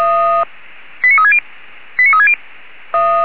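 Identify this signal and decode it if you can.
Five-tone selective call (ZVEI), heard as quick runs of five tones, tones of about 70 ms. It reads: 89360 89360
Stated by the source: Five-tone ZVEI selective call received by SDR.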